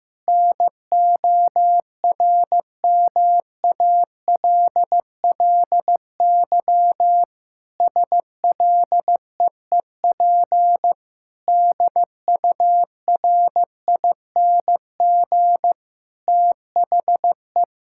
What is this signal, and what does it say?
Morse code, heard as a keyed tone, about 15 wpm, 700 Hz. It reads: NORMALLY SLEEP DURING THE
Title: Sentences from Top 1000 Words - Review Part 6 - 15wpm